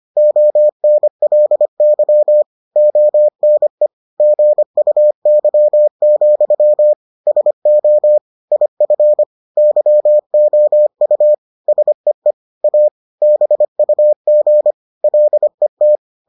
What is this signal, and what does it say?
Morse code, heard as a keyed tone, about 25 wpm, 600 Hz. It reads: ONLY ONE GUY, SO IF YOU SEE A BUG LET